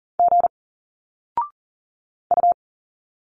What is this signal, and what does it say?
Morse code, heard as a keyed tone, about 40 wpm, 700 Hz. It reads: G U
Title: Single Letters 40wpm